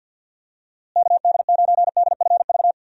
Morse code, heard as a keyed tone, about 50 wpm, 700 Hz. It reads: KD9DFV
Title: USA Call Signs - 6 Characters - Repeated 3x - 50wpm